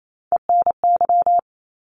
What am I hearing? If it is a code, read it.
Morse code, heard as a keyed tone, about 28 wpm, 700 Hz. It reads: ENY
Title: ARRL Sections 28wpm